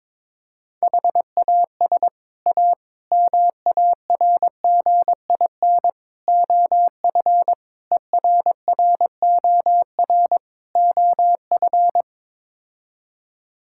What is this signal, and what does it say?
Morse code, heard as a keyed tone, about 22 wpm, 700 Hz. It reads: HAS A MARGIN OF ERROR OF
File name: practice / 22wpm / Sets of 6 Words - 22wpm